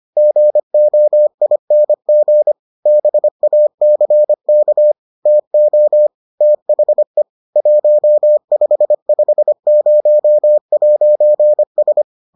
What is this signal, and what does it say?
Morse code, heard as a keyed tone, about 25 wpm, 600 Hz. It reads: GOING BACK TO THE 1550'S